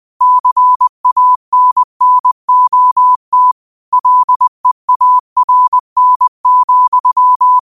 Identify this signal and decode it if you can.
Morse code, heard as a keyed tone, about 20 wpm, 1000 Hz. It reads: CANNOT LEARN,
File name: qotdfiles / 20wpm